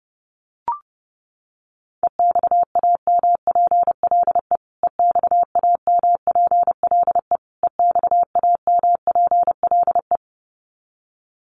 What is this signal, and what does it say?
Morse code, heard as a keyed tone, about 30 wpm, 700 Hz. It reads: EXAMPLE EXAMPLE EXAMPLE
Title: Top 500 Words Repeated 30wpm